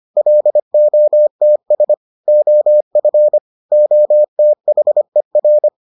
Morse code, heard as a keyed tone, about 25 wpm, 600 Hz. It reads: LOTS OF OTHER